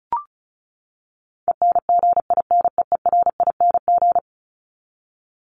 Morse code, heard as a keyed tone, about 34 wpm, 700 Hz. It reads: ENGINEERING
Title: ING Suffix - Top 100 Words with Base Words 35wpm